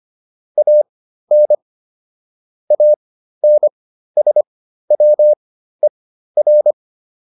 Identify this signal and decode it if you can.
Morse code, heard as a keyed tone, about 25 wpm, spaced as for 13 wpm, 600 Hz. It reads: AN ANSWER